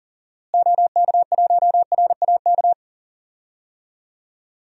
Morse code, heard as a keyed tone, about 37 wpm, 700 Hz. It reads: OK1RAK
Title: DXCC-100 Call Signs - 6 Characters - Repeated 3x - 40wpm